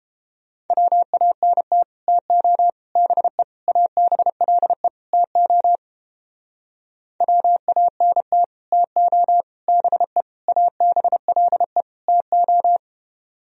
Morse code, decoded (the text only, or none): WANT TO BE ABLE TO WANT TO BE ABLE TO